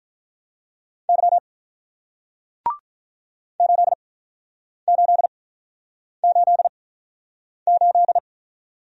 Morse code, heard as a keyed tone, about 50 wpm, spaced as for 9 wpm, 700 Hz. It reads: Y 8888